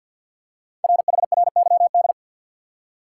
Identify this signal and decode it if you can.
Morse code, heard as a keyed tone, about 48 wpm, 690 Hz. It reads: RHRYD
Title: Code Groups - 5 Letters - 50wpm